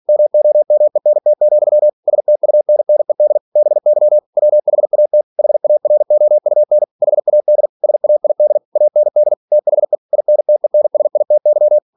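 Morse code, heard as a keyed tone, about 46 wpm, 600 Hz. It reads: MOMENT, STUNNED BY WHAT HAROUN HAD SAID AND THE INTENSITY
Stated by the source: Practice text from an English novel sent at 47 WPM.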